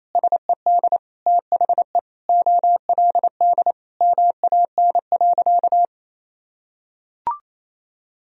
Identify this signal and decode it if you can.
Morse code, heard as a keyed tone, about 28 wpm, 700 Hz. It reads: SED THE OLD MAN.
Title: Sentences from Top 500 Words and Abbreviations 28wpm